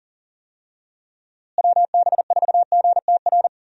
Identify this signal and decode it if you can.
Morse code, heard as a keyed tone, about 40 wpm, 700 Hz. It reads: WB4GTR
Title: USA Call Signs - 6 Characters - Repeated 3x - 40wpm